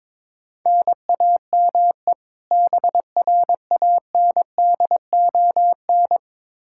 Morse code, heard as a keyed tone, about 21 wpm, 700 Hz. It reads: NAME BRANDON
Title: K1USN SST QSO Practice 22wpm